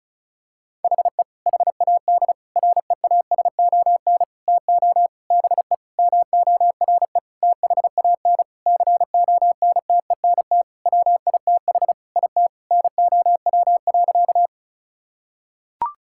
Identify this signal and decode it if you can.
Morse code, heard as a keyed tone, about 36 wpm, 700 Hz. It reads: HE HAD REASON TO BE MORE THAN CONTENT WITH IT NOW.
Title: Sentences from Top 1000 Words - Review Part 6 - 35wpm